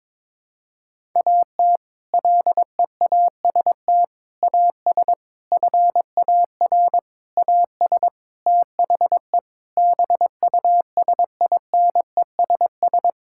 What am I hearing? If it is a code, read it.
Morse code, heard as a keyed tone, about 22 wpm, 700 Hz. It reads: AT LEAST AS FAR AS THE BUSINESS
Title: Sentences from Top 1000 Words - Review Part 6 - 22wpm